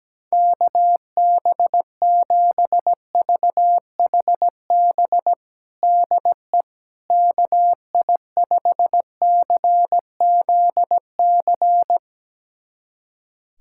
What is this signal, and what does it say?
Morse code, heard as a keyed tone, about 17 wpm, 700 Hz. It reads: KB7VHB DE KI5CZC
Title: USA Callsign DE Callsign - 6 Characters - 17wpm